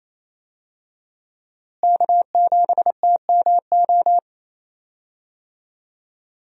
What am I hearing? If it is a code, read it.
Morse code, heard as a keyed tone, about 28 wpm, 700 Hz. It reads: K7TMO
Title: USA Call Signs - 5 Characters - Repeated 2x - 28wpm